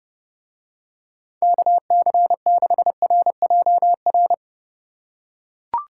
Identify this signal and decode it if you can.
Morse code, heard as a keyed tone, about 29 wpm, 700 Hz. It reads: KC6RJR